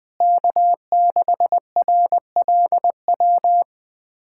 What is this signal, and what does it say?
Morse code, heard as a keyed tone, about 20 wpm, 700 Hz. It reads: K6RLW
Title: USA Callsign DE Callsign - 5 Characters - 20wpm